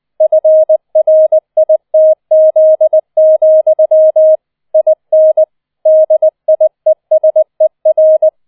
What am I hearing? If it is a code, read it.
Morse code, heard as a keyed tone, about 20 wpm, 630 Hz. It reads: FRITZ, IN DIESER